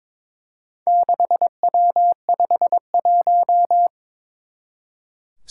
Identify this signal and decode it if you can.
Morse code, heard as a keyed tone, about 22 wpm, 700 Hz. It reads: 6W51
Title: Four Challenging Letter-Number Combinations 22wpm